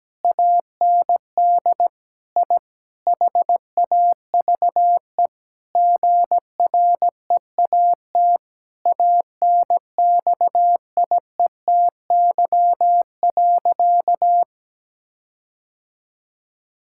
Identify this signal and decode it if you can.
Morse code, heard as a keyed tone, about 17 wpm, 700 Hz. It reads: AND I HAVE GREAT ANXIETY.
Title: Sentences from Top 1600 Words 17wpm